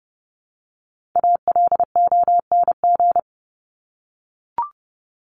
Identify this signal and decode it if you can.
Morse code, heard as a keyed tone, about 30 wpm, 700 Hz. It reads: ALONG
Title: Top 501-750 Words 30wpm